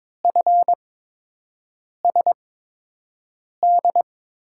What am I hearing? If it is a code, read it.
Morse code, heard as a keyed tone, about 22 wpm, spaced as for 6 wpm, 700 Hz. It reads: FSD
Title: USA Call Signs - 8x Character Spacing - 22wpm